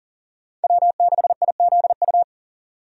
Morse code, heard as a keyed tone, about 40 wpm, 700 Hz. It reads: W6IZU
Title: USA Call Signs - 5 Characters - 40wpm